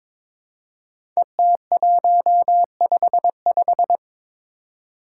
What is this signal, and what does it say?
Morse code, heard as a keyed tone, about 23 wpm, 700 Hz. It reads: ET155
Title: Five Challenging Letter-Number Combinations 22wpm